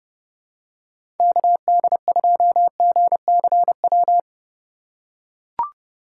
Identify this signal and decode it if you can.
Morse code, heard as a keyed tone, about 32 wpm, 700 Hz. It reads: KD2GCW
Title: USA Call Signs - 6 Characters - 2x Character Spacing - 30wpm